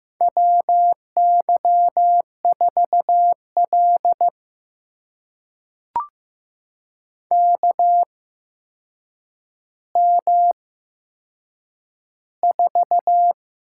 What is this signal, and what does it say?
Morse code, heard as a keyed tone, about 15 wpm, spaced as for 7 wpm, 700 Hz. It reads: WY4L KM4
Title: USA Call Signs - 8x Character Spacing - 15wpm